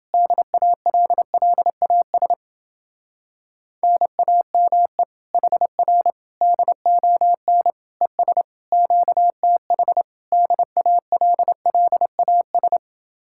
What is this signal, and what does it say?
Morse code, heard as a keyed tone, about 29 wpm, 700 Hz. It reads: DALLAS NAME HR DON ES QTH DALLAS